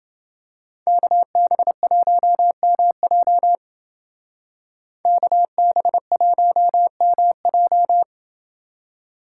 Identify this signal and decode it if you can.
Morse code, heard as a keyed tone, about 29 wpm, 700 Hz. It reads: KB1MJ KB1MJ